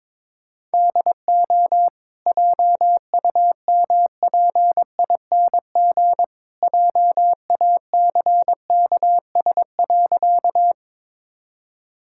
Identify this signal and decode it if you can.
Morse code, heard as a keyed tone, about 22 wpm, 690 Hz. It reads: DO JUMPING JACKS.
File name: practice / 22wpm / Easy Sentences with Hints - 22wpm